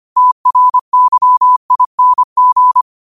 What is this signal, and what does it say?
Morse code, heard as a keyed tone, about 25 wpm, 1000 Hz. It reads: TRYING
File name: qotdfiles / 25wpm